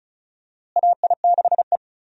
Morse code, heard as a keyed tone, about 36 wpm, 700 Hz. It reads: AI6E
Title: USA Call Signs - 4 Characters - Repeated 3x - 35wpm